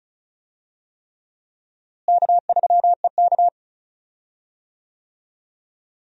Morse code, heard as a keyed tone, about 36 wpm, 700 Hz. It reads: K3EK
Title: USA Call Signs - 4 Characters - Repeated 3x - 35wpm